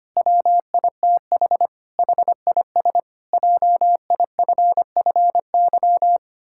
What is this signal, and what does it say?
Morse code, heard as a keyed tone, about 25 wpm, 700 Hz. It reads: WITH HIS JIFFY
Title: Pangram Challenge v2 - 25wpm